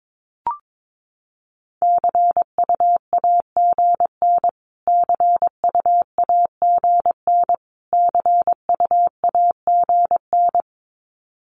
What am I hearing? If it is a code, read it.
Morse code, heard as a keyed tone, about 22 wpm, 700 Hz. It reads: CUAGN CUAGN CUAGN
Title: Abbreviations Repeated 3x - 22wpm